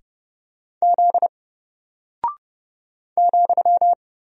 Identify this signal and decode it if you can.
Morse code, heard as a keyed tone, about 30 wpm, 700 Hz. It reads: Z ,
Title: Single Character (Letter:Number:Punctuation:Prosign) Alt 30wpm